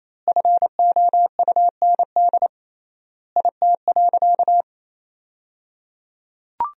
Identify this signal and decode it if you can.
Morse code, heard as a keyed tone, about 27 wpm, 700 Hz. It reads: FOUND IT.
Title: 3x Word-Spacing - Sentences from Top 300 Words - 28wpm